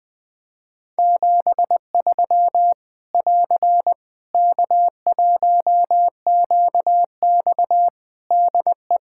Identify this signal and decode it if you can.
Morse code, heard as a keyed tone, about 20 wpm, 700 Hz. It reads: 73 <AR> K1QX DE
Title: K1USN SST QSO Practice 20wpm